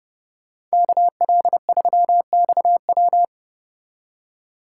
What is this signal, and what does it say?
Morse code, heard as a keyed tone, about 34 wpm, 700 Hz. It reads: KL3XW